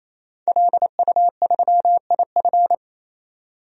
Morse code, heard as a keyed tone, about 28 wpm, 700 Hz. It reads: LU3IF
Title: DXCC-100 Call Signs - 5 Characters - 28wpm